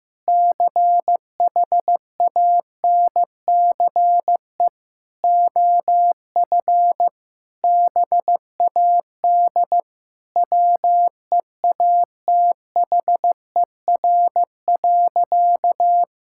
Morse code, heard as a keyed tone, about 15 wpm, 700 Hz. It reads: CHANCE OF BAD WEATHER.
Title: Sentences from Top 1000 Words - Review Part 6 - 15wpm